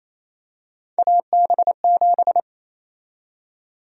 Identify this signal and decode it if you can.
Morse code, heard as a keyed tone, about 28 wpm, 700 Hz. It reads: AB7
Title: USA Call Sign Prefixes - 28wpm